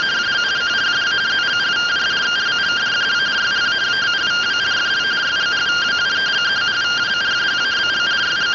Two-tone teleprinter signal, about 170 Hz shift.